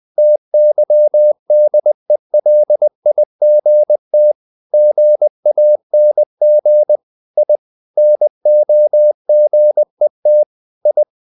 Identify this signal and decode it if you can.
Morse code, heard as a keyed tone, about 20 wpm, 600 Hz. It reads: TYDELIGT GANG I NOGET I
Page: Ferie 20 wpm